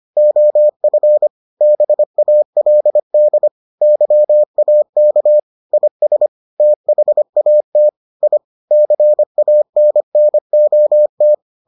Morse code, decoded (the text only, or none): OF BALD YAK IS THAT I CANNOT